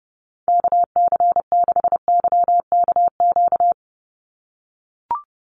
Morse code, heard as a keyed tone, about 30 wpm, 700 Hz. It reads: KC6YKQ
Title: USA Call Signs 30wpm